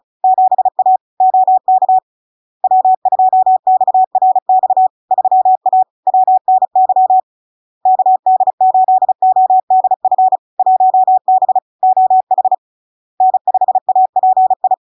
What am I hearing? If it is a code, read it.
Morse code, heard as a keyed tone, about 36 wpm, 750 Hz. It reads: 7A OK W2XRX 3A WNY KD8ODF 1B OH N5API
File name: LARK-FD-2022-Log-in-CW-35wpm-750-Hz